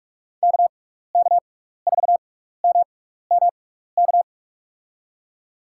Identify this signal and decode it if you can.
Morse code, heard as a keyed tone, about 46 wpm, spaced as for 15 wpm, 690 Hz. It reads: KK4MMK
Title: USA Call Signs - 6x Character Spacing - 45wpm